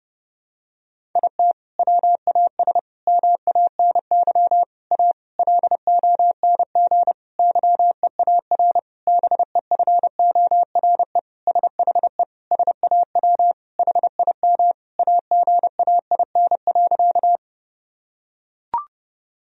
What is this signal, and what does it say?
Morse code, heard as a keyed tone, about 30 wpm, 700 Hz. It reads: IT WAS MANY A LONG YEAR BEFORE SHE SAW HIM AGAIN.
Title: Sentences from Top 1000 Words - Review Part 1 - 30wpm